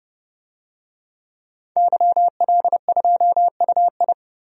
Morse code, heard as a keyed tone, about 30 wpm, 700 Hz. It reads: YL2UI
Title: DXCC-100 Call Signs - 5 Characters - 30wpm